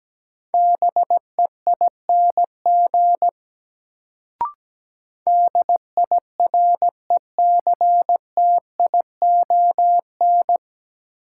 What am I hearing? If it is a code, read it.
Morse code, heard as a keyed tone, about 17 wpm, 700 Hz. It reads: BEING DIRECTION
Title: Sets of 4 Words - New Word Drill - 17wpm